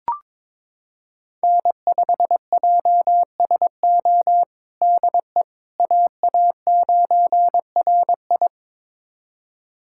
Morse code, heard as a keyed tone, about 22 wpm, 700 Hz. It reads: N5JSO DE AA9RI
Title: USA Callsign DE Callsign - 5 Characters - 22wpm